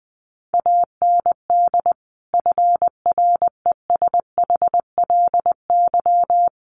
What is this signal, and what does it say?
Morse code, heard as a keyed tone, about 20 wpm, 700 Hz. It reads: AND FRESHLY
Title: Sets of 5 Words Encore 20wpm